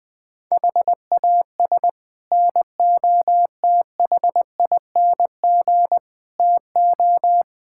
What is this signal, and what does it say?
Morse code, heard as a keyed tone, about 19 wpm, 690 Hz. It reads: HAS NOTHING TO
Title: Build A Phrase - 5-Word Phrases - 20wpm